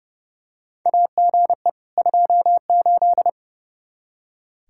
Morse code, heard as a keyed tone, about 30 wpm, 700 Hz. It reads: AGE 28
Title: Practice QSO - Exchange 3 - More Info - 30wpm